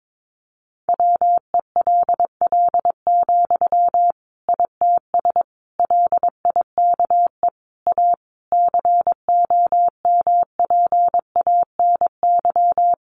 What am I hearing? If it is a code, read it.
Morse code, heard as a keyed tone, about 22 wpm, 700 Hz. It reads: WELL, ITS LIKE A COMPANY
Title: Sentences from Top 600 Words 22wpm